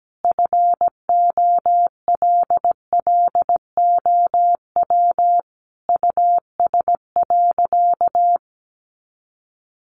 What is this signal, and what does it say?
Morse code, heard as a keyed tone, about 17 wpm, 700 Hz. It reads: FOLLOW US.
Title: Sentences from Top 1500 Words 17wpm